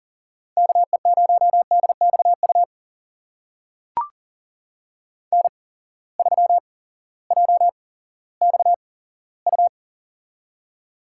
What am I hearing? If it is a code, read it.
Morse code, heard as a keyed tone, about 40 wpm, spaced as for 18 wpm, 700 Hz. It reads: KE0DXU N3JXU
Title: USA Call Signs - 8x Character Spacing - 40wpm